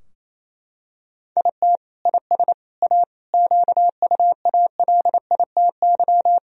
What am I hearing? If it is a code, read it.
Morse code, heard as a keyed tone, about 27 wpm, 700 Hz. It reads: IT IS A QUALITY
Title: Sentences from Top 1000 Words - Review Part 4 - 28wpm